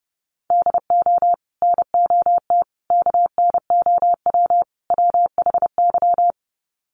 Morse code, heard as a keyed tone, about 28 wpm, 700 Hz. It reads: DO NOT KNOW WHY
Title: Sets of 4 Words 30wpm